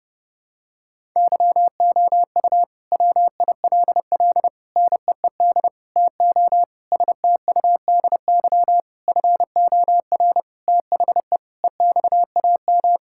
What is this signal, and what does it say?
Morse code, heard as a keyed tone, about 32 wpm, 700 Hz. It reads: YOU WILL NEED TO STUDY FOR THE EXAM
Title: Sentences from Top 1000 Words - Review Part 5 - 30wpm